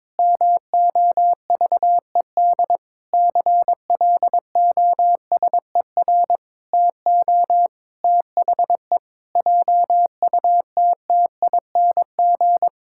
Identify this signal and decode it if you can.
Morse code, heard as a keyed tone, about 22 wpm, 700 Hz. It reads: MOVED CLOSER TO THE JUTTING
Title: Pangram Challenge v2 - 22wpm